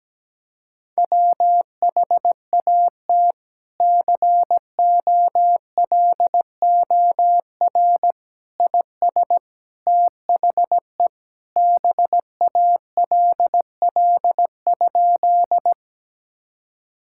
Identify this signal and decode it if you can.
Morse code, heard as a keyed tone, about 17 wpm, 700 Hz. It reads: WHAT COLOR IS THE BALL?
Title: Sentences from Top 1000 Words - Review Part 6 - 17wpm